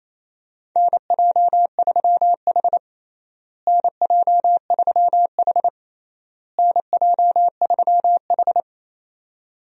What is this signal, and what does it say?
Morse code, heard as a keyed tone, about 27 wpm, 700 Hz. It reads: NJ3H NJ3H NJ3H
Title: USA Call Signs - 4 Characters - Repeated 3x - 28wpm